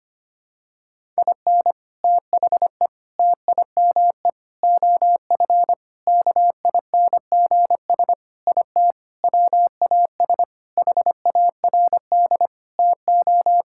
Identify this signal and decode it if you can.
Morse code, heard as a keyed tone, about 25 wpm, 700 Hz. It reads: IN THE TIME OF KINGS IT WAS HARD TO
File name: practice / 25wpm / Sentences with 3-Word Phrases - 25wpm